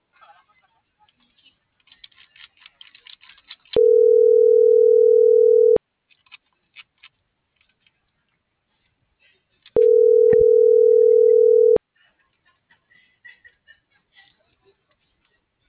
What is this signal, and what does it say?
Ringback tone (North America), 440 + 480 Hz.